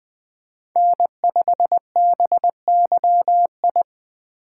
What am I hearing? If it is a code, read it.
Morse code, heard as a keyed tone, about 20 wpm, 700 Hz. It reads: N5BYI
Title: USA Call Signs - 5 Characters - 6x Character Spacing - 20wpm